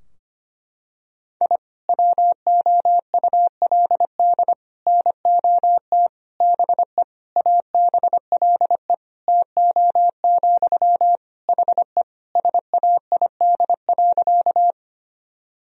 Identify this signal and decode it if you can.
Morse code, heard as a keyed tone, about 25 wpm, 700 Hz. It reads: I WOULD NOT BE ABLE TO, HE SAID.
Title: Sentences from Top 1000 Words - Review Part 2 - 25wpm